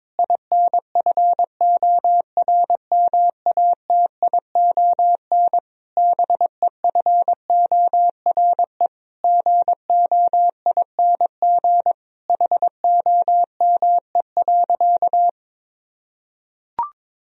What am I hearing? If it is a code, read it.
Morse code, heard as a keyed tone, about 22 wpm, 700 Hz. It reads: INFORMATION BEFORE GOING HOME.
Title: Sentences from Top 1000 Words - Review Part 3 - 22wpm